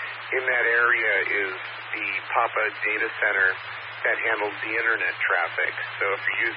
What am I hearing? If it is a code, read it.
Emergency Alert System (SAME) tones (header bursts), heard as harsh buzzing bursts, with speech. It is heard from 0.7 to 5.7 s.